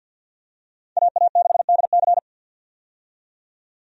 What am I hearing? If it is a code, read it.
Morse code, heard as a keyed tone, about 46 wpm, 700 Hz. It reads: AA6DC
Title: USA Call Signs - 5 Characters - 50wpm